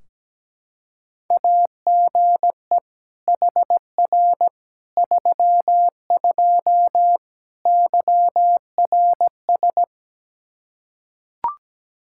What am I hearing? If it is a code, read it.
Morse code, heard as a keyed tone, about 17 wpm, 700 Hz. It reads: AGE HR 32 YRS
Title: Practice QSO - Exchange 3 - More Info - 17wpm